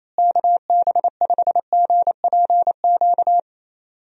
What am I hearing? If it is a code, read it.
Morse code, heard as a keyed tone, about 28 wpm, 700 Hz. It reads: KB5GPQ
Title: USA Call Signs - 6 Characters - Repeated 3x - 28wpm